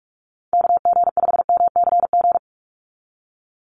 Morse code, heard as a keyed tone, about 46 wpm, 700 Hz. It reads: KG5MCG
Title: USA Call Signs 45wpm